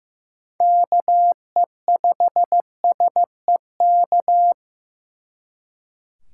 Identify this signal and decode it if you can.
Morse code, heard as a keyed tone, about 18 wpm, 700 Hz. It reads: KE5SEK
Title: Speed Racing - USA 2x3 Call Signs 15wpm